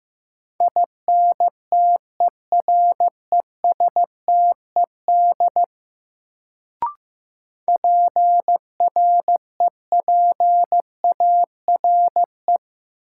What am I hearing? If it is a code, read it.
Morse code, heard as a keyed tone, about 15 wpm, 700 Hz. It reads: INTERESTED PREPARE
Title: Sentences from Top 900 Words - New Word Drill - 15wpm